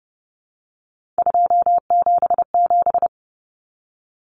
Morse code, heard as a keyed tone, about 30 wpm, 700 Hz. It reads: 277
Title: Common Cut Numbers - 3-Digit 30wpm